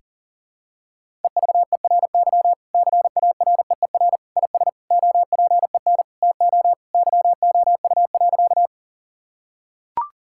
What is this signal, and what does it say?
Morse code, heard as a keyed tone, about 37 wpm, 700 Hz. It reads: EVERY CAREER IS OPEN TO YOU.
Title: Sentences from Top 1000 Words - Review Part 6 - 40wpm